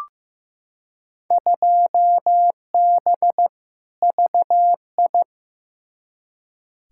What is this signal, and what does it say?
Morse code, heard as a keyed tone, about 15 wpm, 700 Hz. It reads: 2B VI
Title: Field Day Exchanges Encore - 15wpm